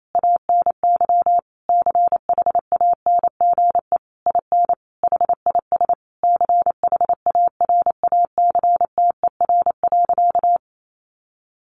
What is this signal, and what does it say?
Morse code, heard as a keyed tone, about 28 wpm, 700 Hz. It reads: ANY CHANGE IN HIS CHARACTER.
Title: Sentences from Top 1200 Words 28wpm